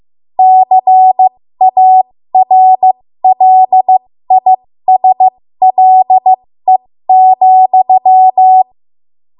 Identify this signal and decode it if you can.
Morse code, heard as a keyed tone, about 15 wpm, spaced as for 13 wpm, 750 Hz. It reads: CARLISLE,
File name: Morse / 13WPM